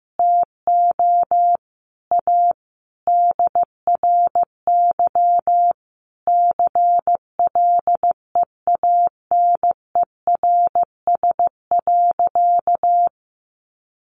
Morse code, heard as a keyed tone, about 15 wpm, 700 Hz. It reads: TO A DRY CLEANERS.
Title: Additional Easy Phrases 15wpm